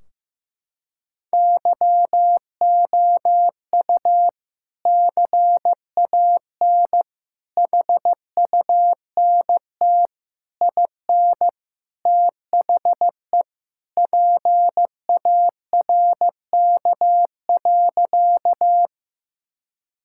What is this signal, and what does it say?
Morse code, heard as a keyed tone, about 15 wpm, 700 Hz. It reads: YOU CAN HUNT IN THE PARK.
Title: Sentences from Top 1600 Words 15wpm